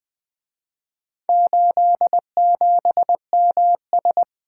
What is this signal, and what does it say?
Morse code, heard as a keyed tone, about 23 wpm, 700 Hz. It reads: 87MS
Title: Speed-Racing - Four Letter-Number Combinations - 20wpm